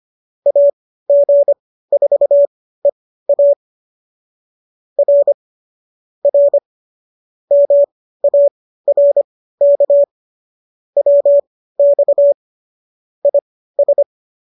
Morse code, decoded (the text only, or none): AG4EA R R MARK WX IS